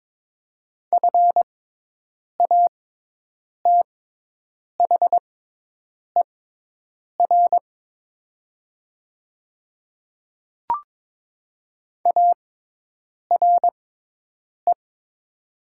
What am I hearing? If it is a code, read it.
Morse code, heard as a keyed tone, about 22 wpm, spaced as for 8 wpm, 700 Hz. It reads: FATHER ARE
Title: 6x Character Spacing - Top 300 Words - 22wpm